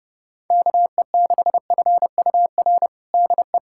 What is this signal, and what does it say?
Morse code, heard as a keyed tone, about 30 wpm, 700 Hz. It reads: KE6FUR DE
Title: USA Callsign DE Callsign 30wpm